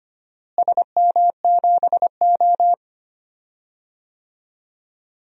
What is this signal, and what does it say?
Morse code, heard as a keyed tone, about 25 wpm, 700 Hz. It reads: SM7O
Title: DXCC-100 Call Signs - 4 Characters - Repeated 3x - 25wpm